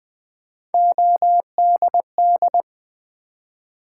Morse code, heard as a keyed tone, about 20 wpm, 700 Hz. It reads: ODD
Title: Build A Word - 1 Letter At A Time - 3-Letter Words - 20wpm